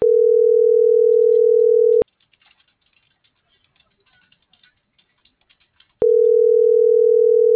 Ringback tone (North America), 440 + 480 Hz.